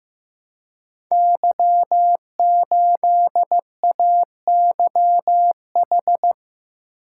Morse code, heard as a keyed tone, about 15 wpm, 700 Hz. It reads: Y8AYH
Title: Code Groups - 5 Letter:Numbers - 15wpm